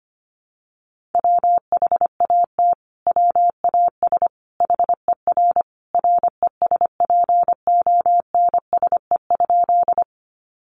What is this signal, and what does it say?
Morse code, heard as a keyed tone, about 25 wpm, 700 Hz. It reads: WHAT WAS HER RESPONSE?
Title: Sentences from Top 1400 Words 25wpm